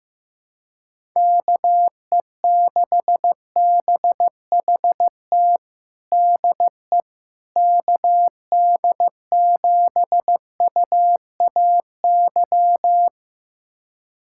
Morse code, heard as a keyed tone, about 16 wpm, 700 Hz. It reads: KE6BHT DE KD7UAY
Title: USA Callsign DE Callsign 15wpm